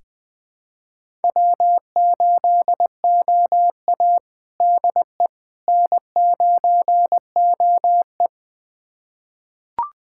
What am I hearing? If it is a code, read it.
Morse code, heard as a keyed tone, about 21 wpm, 700 Hz. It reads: W8OA DE N9OE
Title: USA Callsign DE Callsign - 4 Characters - 20wpm